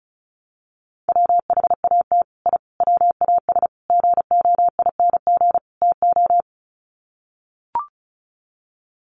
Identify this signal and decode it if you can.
Morse code, heard as a keyed tone, about 34 wpm, 700 Hz. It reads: WHAT I WAS GOING TO
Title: Sets of 5 Words Encore2 35wpm